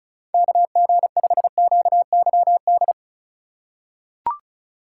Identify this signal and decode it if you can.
Morse code, heard as a keyed tone, about 36 wpm, 700 Hz. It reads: KG5QYD
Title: USA Call Signs - 6 Characters - 2x Character Spacing - 35wpm